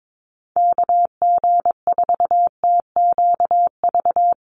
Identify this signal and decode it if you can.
Morse code, heard as a keyed tone, about 22 wpm, 700 Hz. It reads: KG4TQV
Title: USA Call Signs 22wpm